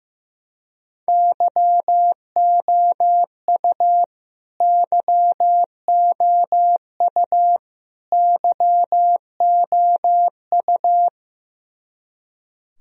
Morse code, heard as a keyed tone, about 15 wpm, 700 Hz. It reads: YOU YOU YOU
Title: Top 100 Words Repeated 15wpm